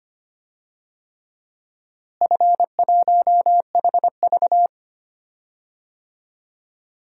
Morse code, heard as a keyed tone, about 25 wpm, 700 Hz. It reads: F1HV
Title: DXCC-100 Call Signs - 4 Characters - Repeated 3x - 25wpm